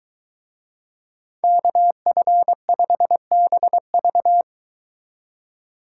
Morse code, heard as a keyed tone, about 23 wpm, 700 Hz. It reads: KF5BV